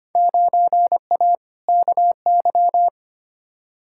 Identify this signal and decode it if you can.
Morse code, heard as a keyed tone, about 25 wpm, 700 Hz. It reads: 9A KY
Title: Field Day Exchanges Encore - 25wpm